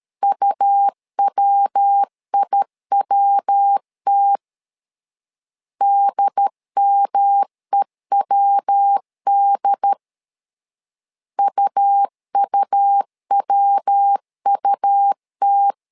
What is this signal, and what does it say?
Morse code, heard as a keyed tone, about 13 wpm, 800 Hz. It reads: UWIWT DMEWD UUWUT
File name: LEZIONI / 3 MWRO / 3 MW 12wpm 1633